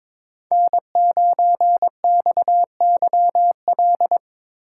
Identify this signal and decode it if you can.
Morse code, heard as a keyed tone, about 26 wpm, 700 Hz. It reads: N9XYL
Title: Speed Racing - USA 1x3 Call Signs 22wpm